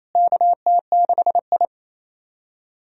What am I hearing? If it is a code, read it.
Morse code, heard as a keyed tone, about 28 wpm, 700 Hz. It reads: KT6I
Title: USA Call Signs - 4 Characters - Repeated 3x - 28wpm